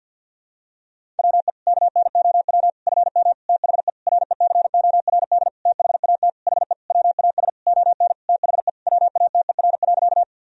Morse code, heard as a keyed tone, about 48 wpm, 700 Hz. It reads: WE KNOW FM THE RECORD THAT HE WAS ON THE WATER.
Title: Sentences from Top 500 Words and Abbreviations 50wpm